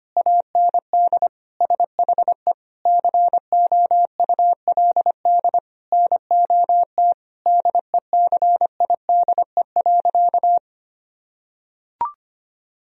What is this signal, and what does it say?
Morse code, heard as a keyed tone, about 25 wpm, 700 Hz. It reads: AND SHE COULD NOT DECIDE.
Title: Sentences from Top 1000 Words - Review Part 3 - 25wpm